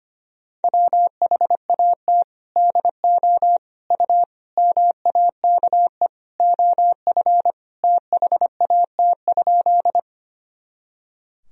Morse code, heard as a keyed tone, about 25 wpm, 700 Hz. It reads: WHAT DO U MAKE OF THAT?
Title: Sentences from Top 100 Words and Abbreviations 25wpm